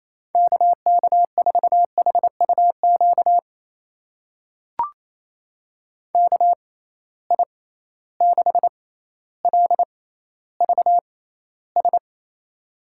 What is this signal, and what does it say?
Morse code, heard as a keyed tone, about 28 wpm, spaced as for 14 wpm, 700 Hz. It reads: KK4HUQ KI6LVS
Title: USA Call Signs - 6 Characters - 6x Character Spacing - 28wpm